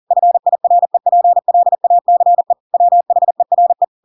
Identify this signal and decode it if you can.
Morse code, heard as a keyed tone, about 40 wpm, 700 Hz. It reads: FIREPLACE WHERE
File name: Pinocchio / Pinocchio ch5 40 40 ik1hge